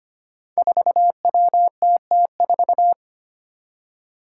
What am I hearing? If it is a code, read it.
Morse code, heard as a keyed tone, about 25 wpm, 700 Hz. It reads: 4WTT4
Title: Five Challenging Letter-Number Combinations 25wpm